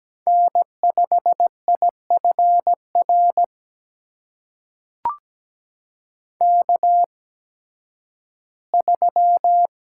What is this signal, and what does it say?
Morse code, heard as a keyed tone, about 17 wpm, spaced as for 11 wpm, 700 Hz. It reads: N5IFR K3
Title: USA Call Signs - 5 Characters - 8x Character Spacing - 17wpm